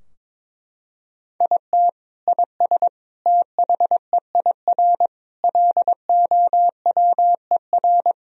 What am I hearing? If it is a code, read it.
Morse code, heard as a keyed tone, about 22 wpm, 700 Hz. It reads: IT IS THEIR LOWER
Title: Sentences from Top 1000 Words - Review Part 4 - 22wpm